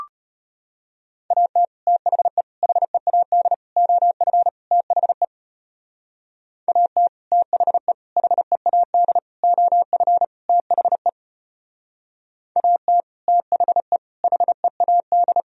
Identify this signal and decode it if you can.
Morse code, heard as a keyed tone, about 31 wpm, 700 Hz. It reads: AT THE HEAD OF THE AT THE HEAD OF THE AT THE HEAD